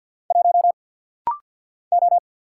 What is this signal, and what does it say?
Morse code, heard as a keyed tone, about 50 wpm, 700 Hz. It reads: 1 O
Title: Single Letter-Number Alt - Rapid-Fire 50wpm